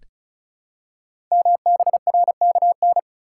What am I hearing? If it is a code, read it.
Morse code, heard as a keyed tone, about 36 wpm, 700 Hz. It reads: MBRKN
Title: Code Groups - 5 Letters - 35wpm